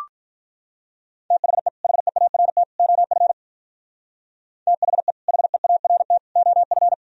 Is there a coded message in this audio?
Yes: Morse code from 1.3 to 6.9 s, heard as a keyed tone, about 44 wpm, 700 Hz. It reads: THE HEART OF THE HEART OF